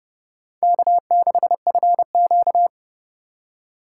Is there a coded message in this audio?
Yes: Morse code from 0.6 to 2.7 s, heard as a keyed tone, about 38 wpm, 700 Hz. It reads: K6FQ